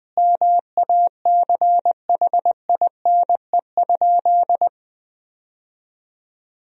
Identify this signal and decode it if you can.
Morse code, heard as a keyed tone, about 20 wpm, 700 Hz. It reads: MACHINE?
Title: Sentences from Top 1000 Words - Review Part 2 - 20wpm